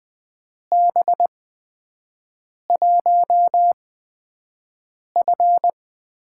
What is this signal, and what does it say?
Morse code, heard as a keyed tone, about 21 wpm, spaced as for 5 wpm, 700 Hz. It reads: B1F
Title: USA Call Signs - 8x Character Spacing - 20wpm